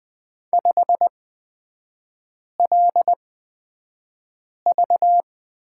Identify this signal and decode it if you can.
Morse code, heard as a keyed tone, about 20 wpm, spaced as for 5 wpm, 700 Hz. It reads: 5LV
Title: 224 Lesson 36e - BT - Three Characters 20wpm